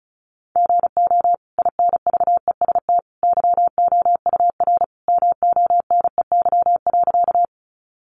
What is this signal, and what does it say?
Morse code, heard as a keyed tone, about 34 wpm, 700 Hz. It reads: GO INVEST YOUR MONEY.
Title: Sentences from Top 1100 Words 35wpm